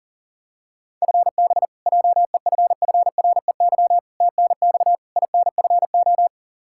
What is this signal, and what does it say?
Morse code, heard as a keyed tone, about 40 wpm, 700 Hz. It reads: FB JEFFREY TNX INFO
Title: Practice QSO - Exchange 3 - More Info - 40wpm